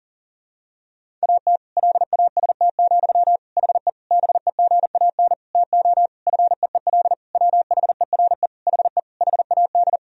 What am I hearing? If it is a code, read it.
Morse code, heard as a keyed tone, about 40 wpm, 700 Hz. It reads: AT LAST, HE BEGAN TO FEEL WHERE HE HAD
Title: Sentences from Top 1000 Words - Review Part 1 - 40wpm